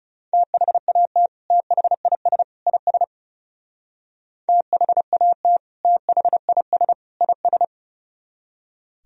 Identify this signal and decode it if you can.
Morse code, heard as a keyed tone, about 35 wpm, 700 Hz. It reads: THAT THIS IS THAT THIS IS
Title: Speed-Racing - Sets of 3 Words 30wpm